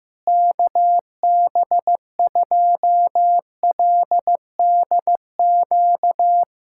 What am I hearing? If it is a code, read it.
Morse code, heard as a keyed tone, about 15 wpm, 700 Hz. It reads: KB2LDQ
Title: USA Callsign DE Callsign - 6 Characters - 15wpm